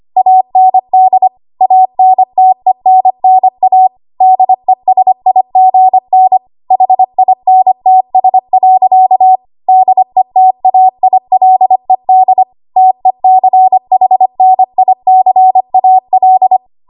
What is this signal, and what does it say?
Morse code, heard as a keyed tone, about 25 wpm, 750 Hz. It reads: AND ANTENNA DESIGN HINTS. DETAILED TECHNICAL
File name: Archive / 25 WPM / 25WPM